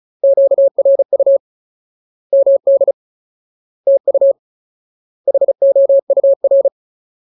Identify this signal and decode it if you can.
Morse code, heard as a keyed tone, about 34 wpm, 550 Hz. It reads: QRU MD TU HOUR
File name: Ultimate Word List / 35WPM Ultimate